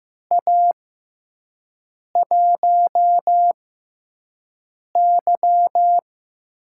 Morse code, heard as a keyed tone, about 15 wpm, spaced as for 5 wpm, 700 Hz. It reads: A1Y
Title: USA Call Signs - 5 Characters - 6x Character Spacing - 15wpm